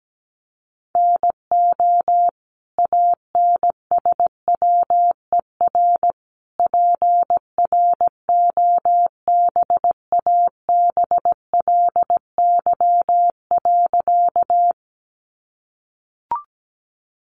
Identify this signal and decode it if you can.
Morse code, heard as a keyed tone, about 17 wpm, 700 Hz. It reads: NO ANSWER PROBABLY.
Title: Sentences from Top 700 Words 17wpm